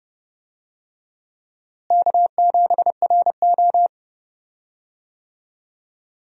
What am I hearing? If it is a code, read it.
Morse code, heard as a keyed tone, about 31 wpm, 700 Hz. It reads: K7RO
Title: USA Call Signs - 4 Characters - Repeated 2x - 30wpm